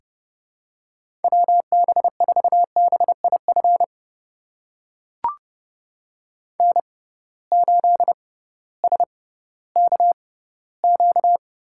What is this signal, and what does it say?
Morse code, heard as a keyed tone, about 29 wpm, spaced as for 16 wpm, 700 Hz. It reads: WB4BIF N8SKQ
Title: USA Call Signs - 6x Character Spacing - 30wpm